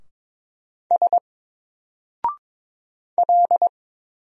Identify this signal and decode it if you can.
Morse code, heard as a keyed tone, about 22 wpm, 700 Hz. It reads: S L
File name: Single-Letter-Number-22wpm